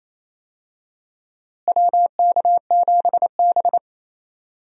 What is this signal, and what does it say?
Morse code, heard as a keyed tone, about 28 wpm, 700 Hz. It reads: WK7B
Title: USA Call Signs - 4 Characters - Repeated 3x - 28wpm